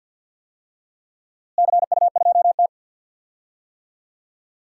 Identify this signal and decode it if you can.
Morse code, heard as a keyed tone, about 50 wpm, 700 Hz. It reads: CU2T
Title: DXCC-100 Call Signs - 4 Characters - Repeated 3x - 50wpm